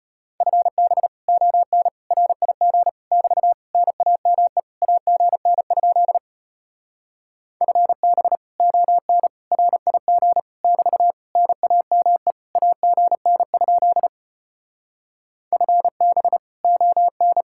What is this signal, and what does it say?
Morse code, heard as a keyed tone, about 33 wpm, 700 Hz. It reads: FB ON RIG <BT> NAME AGN? FB ON RIG <BT> NAME AGN? FB ON